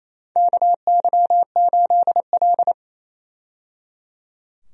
Morse code, heard as a keyed tone, about 28 wpm, 700 Hz. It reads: KY8L
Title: USA Call Signs - 4 Characters - 28wpm